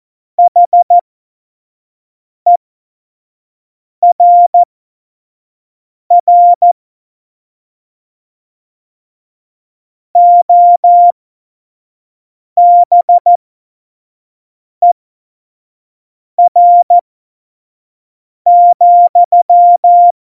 Morse code, heard as a keyed tone, about 14 wpm, spaced as for 5 wpm, 700 Hz. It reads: HERR OBER,
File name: Morsekurs / Witze / Witz 12 Mai 2020, 25 F70